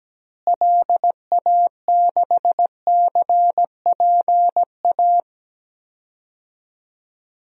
Morse code, heard as a keyed tone, about 17 wpm, 700 Hz. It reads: LA6CPA
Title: DXCC-100 Call Signs - 6 Characters - Repeated 2x - 17wpm